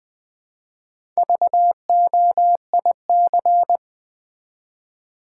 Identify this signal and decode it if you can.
Morse code, heard as a keyed tone, about 20 wpm, 700 Hz. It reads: VOIC
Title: Build A Word - 1 Letter At A Time - 5-Letter Words - 20wpm